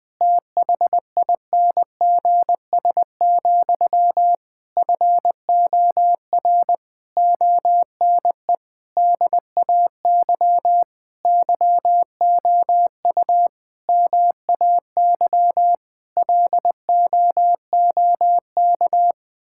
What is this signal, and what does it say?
Morse code, heard as a keyed tone, about 21 wpm, 700 Hz. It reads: THINGS, FOR ONE DAY YOU MAY LOOK
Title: Difficult Sentences with Hints - 20wpm